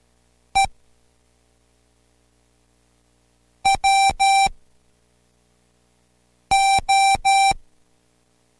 Morse code, heard as a keyed tone, about 12 wpm, spaced as for 4 wpm, 780 Hz. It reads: E WO